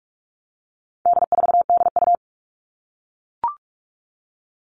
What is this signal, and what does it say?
Morse code, heard as a keyed tone, about 44 wpm, 700 Hz. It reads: D4DU
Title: All Cut Numbers - 4-Digit 45wpm